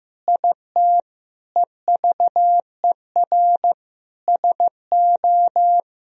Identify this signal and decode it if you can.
Morse code, heard as a keyed tone, about 16 wpm, 700 Hz. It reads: IT EVER SO
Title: Sentences from Top 400 Words and Abbreviations 15wpm